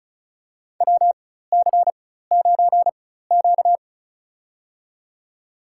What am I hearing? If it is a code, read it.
Morse code, heard as a keyed tone, about 34 wpm, spaced as for 16 wpm, 700 Hz. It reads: WC9Q
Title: USA Call Signs - 4 Characters - 4x Character Spacing - 35wpm